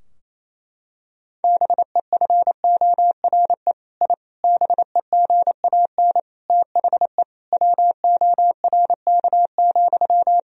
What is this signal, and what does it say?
Morse code, heard as a keyed tone, about 28 wpm, 700 Hz. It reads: BEFORE I BEGAN THE WORK,
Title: Sentences from Top 1000 Words - Review Part 4 - 28wpm